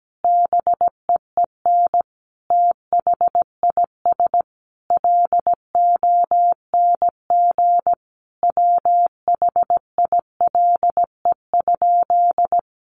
Morse code, heard as a keyed tone, about 17 wpm, 700 Hz. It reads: BEEN THIS LONG WHILE?
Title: Sentences from Top 300 Words 17wpm